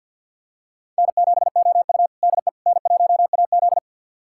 Morse code, heard as a keyed tone, about 53 wpm, 700 Hz. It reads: N7OU DE N1AZ
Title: USA Callsign DE Callsign - 4 Characters - 50wpm